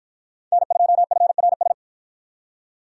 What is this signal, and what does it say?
Morse code, heard as a keyed tone, about 53 wpm, 700 Hz. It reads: N2FRS